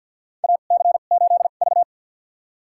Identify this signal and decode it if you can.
Morse code, heard as a keyed tone, about 46 wpm, spaced as for 35 wpm, 700 Hz. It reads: AC8V
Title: USA Call Signs - 4 Characters - 2x Character Spacing - 50wpm